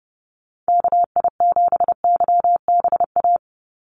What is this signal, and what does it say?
Morse code, heard as a keyed tone, about 30 wpm, 700 Hz. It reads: KI7YBA
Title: USA Call Signs 30wpm